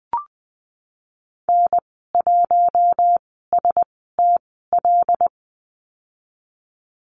Morse code, heard as a keyed tone, about 20 wpm, spaced as for 15 wpm, 700 Hz. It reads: N1STL
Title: 088 Lesson 15g - Number 5 - Call Signs 20wpm